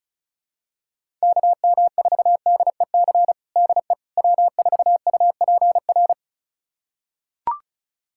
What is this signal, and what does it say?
Morse code, heard as a keyed tone, about 34 wpm, 700 Hz. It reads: KM4DEC DE W4UPR
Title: USA Callsign DE Callsign 35wpm